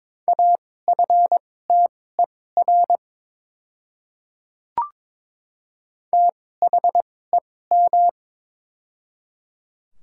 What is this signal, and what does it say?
Morse code, heard as a keyed tone, about 22 wpm, spaced as for 16 wpm, 700 Hz. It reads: AFTER THEM
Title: 2x Character Spacing - Top 300 Words - 22wpm